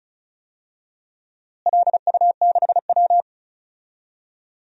Morse code, heard as a keyed tone, about 34 wpm, 700 Hz. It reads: LU6W
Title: DXCC-100 Call Signs - 4 Characters - Repeated 2x - 35wpm